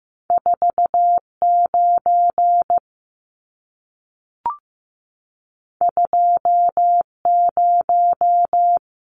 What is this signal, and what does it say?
Morse code, heard as a keyed tone, about 15 wpm, 700 Hz. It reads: 49 20
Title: Numbers - 2-Digit 15wpm